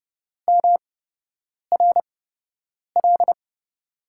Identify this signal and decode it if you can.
Morse code, heard as a keyed tone, about 30 wpm, spaced as for 8 wpm, 700 Hz. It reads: MRL
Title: USA Call Signs - 8x Character Spacing - 30wpm